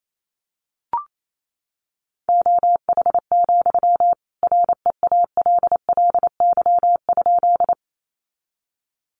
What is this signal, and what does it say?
Morse code, heard as a keyed tone, about 28 wpm, 700 Hz. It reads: OH, REALLY?
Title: Sentences from Top 600 Words 28wpm